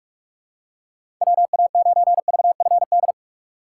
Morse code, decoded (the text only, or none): WA9VFD